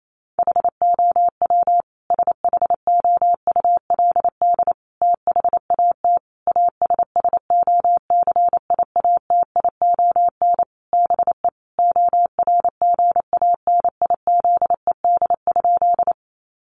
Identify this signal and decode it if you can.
Morse code, heard as a keyed tone, about 28 wpm, 700 Hz. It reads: HOW SHOULD THAT ASSOCIATION BE ORGANIZED?
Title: Sentences from Top 1300 Words 28wpm